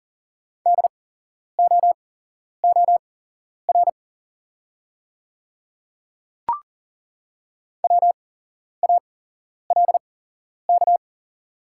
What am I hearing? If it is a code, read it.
Morse code, heard as a keyed tone, about 40 wpm, spaced as for 11 wpm, 690 Hz. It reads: DOOR WALK
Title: 8x Character Spacing - Top 300 Words - 40wpm